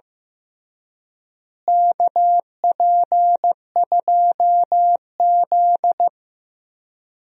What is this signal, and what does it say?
Morse code, heard as a keyed tone, about 15 wpm, 700 Hz. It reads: KP2Z
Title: DXCC-100 Call Signs - 4 Characters - Repeated 3x - 15wpm